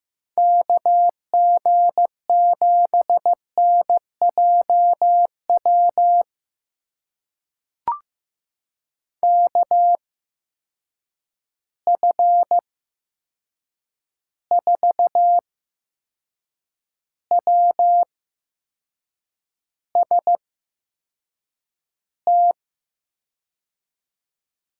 Morse code, heard as a keyed tone, about 15 wpm, spaced as for 6 wpm, 700 Hz. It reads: KG7NJW KF4WST
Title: USA Call Signs - 6 Characters - 8x Character Spacing - 15wpm